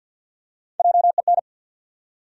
Morse code, heard as a keyed tone, about 46 wpm, 700 Hz. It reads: JEN
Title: USA Common Names - Max 4 Letters - 50wpm